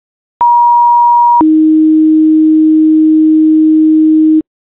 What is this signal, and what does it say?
Two-tone page, heard as a one-second tone, then a three-second tone, about 950 then 310 Hz. It is heard from 0.4 to 4.4 s.